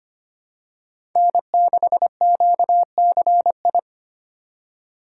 Morse code, heard as a keyed tone, about 25 wpm, 700 Hz. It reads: N6QCI
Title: USA Call Signs - 5 Characters - 3x Character Spacing - 25wpm